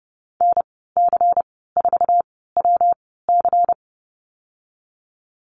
Morse code, heard as a keyed tone, about 30 wpm, spaced as for 17 wpm, 700 Hz. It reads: NC4WC
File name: Lesson 19g - Letter w - Call Signs 30wpm